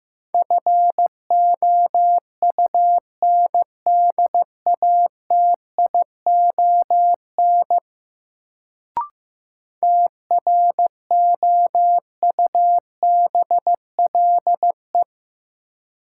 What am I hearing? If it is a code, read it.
Morse code, heard as a keyed tone, about 15 wpm, 700 Hz. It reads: FOUNDATION TROUBLE
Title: Sentences from Top 1200 Words - New Word Drill Alt - 15wpm